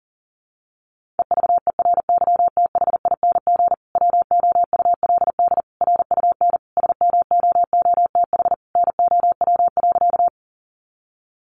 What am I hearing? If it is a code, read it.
Morse code, heard as a keyed tone, about 40 wpm, 700 Hz. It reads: EVERYTHING WOULD RUN SMOOTH NOW.
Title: Sentences from Top 1400 Words 40wpm